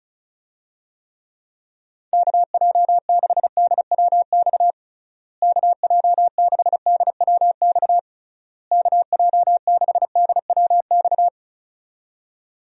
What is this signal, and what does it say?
Morse code, heard as a keyed tone, about 36 wpm, 700 Hz. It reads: KJ6DWX KJ6DWX KJ6DWX
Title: USA Call Signs - 6 Characters - Repeated 3x - 35wpm